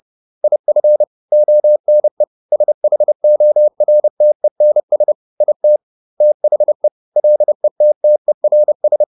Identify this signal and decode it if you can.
Morse code, heard as a keyed tone, about 30 wpm, 600 Hz. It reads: IF ONE SHORTENS IT THE LETTERS